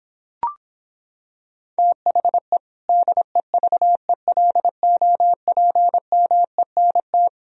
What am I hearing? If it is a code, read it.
Morse code, heard as a keyed tone, about 26 wpm, 700 Hz. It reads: THE DEVELOPMENT